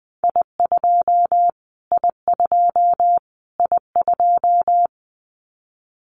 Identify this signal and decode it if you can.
Morse code, heard as a keyed tone, about 20 wpm, 700 Hz. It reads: I2 I2 I2
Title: Callsign Prefixes for Top 25 DXCC - Repeated 3x - 20wpm